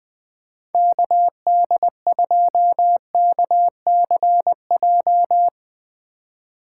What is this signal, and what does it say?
Morse code, heard as a keyed tone, about 20 wpm, 700 Hz. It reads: KD2KCJ
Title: USA Call Signs - 6 Characters - 2x Character Spacing - 20wpm